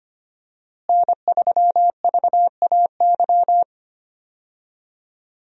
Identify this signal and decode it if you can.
Morse code, heard as a keyed tone, about 25 wpm, 700 Hz. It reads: N3VAY
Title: USA Call Signs - 5 Characters - Repeated 2x - 25wpm